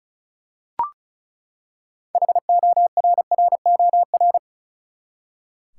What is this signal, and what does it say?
Morse code, heard as a keyed tone, about 36 wpm, 700 Hz. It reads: HORROR
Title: Six Letter Words - 35wpm